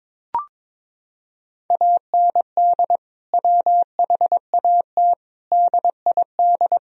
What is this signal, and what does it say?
Morse code, heard as a keyed tone, about 22 wpm, 700 Hz. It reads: AND WHAT DID
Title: Sentences from Top 1000 Words - Review Part 4 - 22wpm